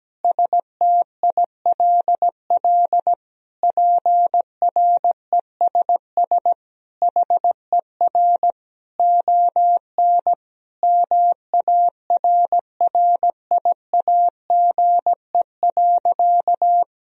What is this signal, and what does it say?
Morse code, heard as a keyed tone, about 17 wpm, 700 Hz. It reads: STILL PRESS HER ON MARRIAGE.
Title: Sentences from Top 1000 Words - Review Part 1 - 17wpm